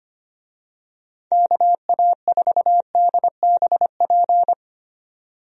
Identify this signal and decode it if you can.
Morse code, heard as a keyed tone, about 25 wpm, 700 Hz. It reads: KA4DBP
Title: USA Call Signs - 6 Characters - 25wpm